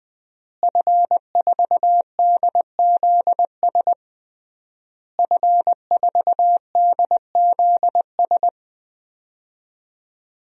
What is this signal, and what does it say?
Morse code, heard as a keyed tone, about 20 wpm, 700 Hz. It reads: F4DZS F4DZS
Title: DXCC-100 Call Signs - 5 Characters - Repeated 3x - 20wpm